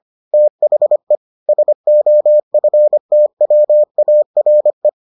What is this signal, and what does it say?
Morse code, heard as a keyed tone, about 25 wpm, 600 Hz. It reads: THE SOFTWARE